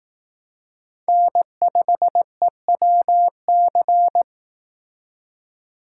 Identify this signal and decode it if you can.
Morse code, heard as a keyed tone, about 18 wpm, 700 Hz. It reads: N5EWC
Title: Speed Racing - USA 1x3 Call Signs 15wpm